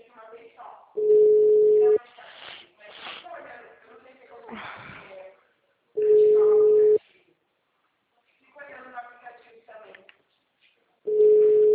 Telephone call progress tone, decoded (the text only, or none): ringback tone